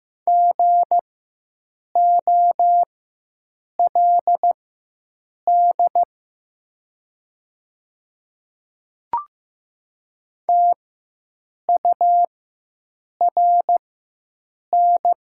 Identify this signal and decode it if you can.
Morse code, heard as a keyed tone, about 15 wpm, spaced as for 7 wpm, 690 Hz. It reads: GOLD TURN
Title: 4x Character Spacing - Top 500 Words - 15wpm